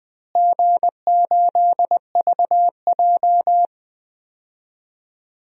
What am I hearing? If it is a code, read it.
Morse code, heard as a keyed tone, about 20 wpm, 700 Hz. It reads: G8VJ
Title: DXCC-100 Call Signs - 4 Characters - Repeated 3x - 20wpm